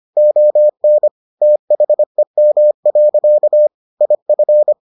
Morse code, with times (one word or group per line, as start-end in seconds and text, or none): 0.2-1.1 ON
1.4-3.7 THEM.
4.0-4.7 IF